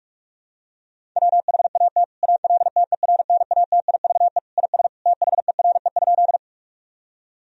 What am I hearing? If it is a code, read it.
Morse code, heard as a keyed tone, about 46 wpm, 700 Hz. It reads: WHAT ALTERNATIVE IS THERE?
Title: Sentences from Top 1000 Words - Review Part 4 - 45wpm